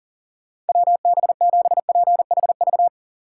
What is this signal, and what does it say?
Morse code, heard as a keyed tone, about 48 wpm, 700 Hz. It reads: WB7PHV
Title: Speed Racing - USA 2x3 Call Signs 40wpm